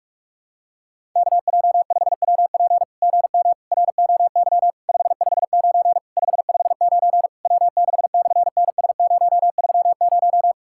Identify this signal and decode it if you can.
Morse code, heard as a keyed tone, about 48 wpm, 700 Hz. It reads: KJ5WP GM ROY 559 559 W6/NS030
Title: SOTA QSO Practice 45wpm